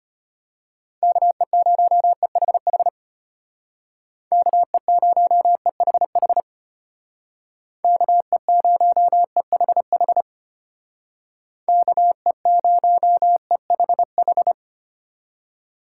Morse code, heard as a keyed tone, about 32 wpm, 700 Hz. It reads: KE0EHH KE0EHH KE0EHH KE0EHH